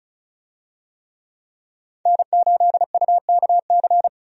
Morse code, heard as a keyed tone, about 36 wpm, 700 Hz. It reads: N8UKC
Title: USA Call Signs - 5 Characters - 35wpm